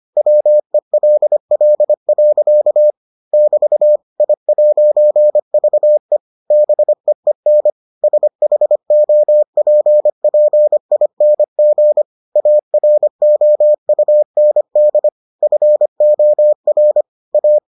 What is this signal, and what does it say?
Morse code, heard as a keyed tone, about 25 wpm, 600 Hz. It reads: WELL. <BT> I'VE BEEN SHOPPING AROUND FOR A